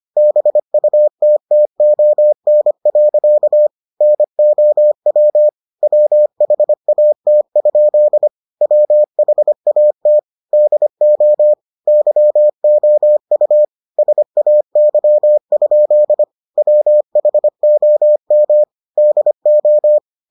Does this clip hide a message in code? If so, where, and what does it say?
Yes: Morse code from 0.2 to 20.0 s, heard as a keyed tone, about 25 wpm, 600 Hz. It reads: BUTTON. NOW WHAT? WHAT DO YOU SAY? WHOM DO